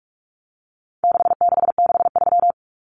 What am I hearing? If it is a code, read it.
Morse code, heard as a keyed tone, about 44 wpm, 700 Hz. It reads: 6663
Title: Numbers - 4-Digit 45wpm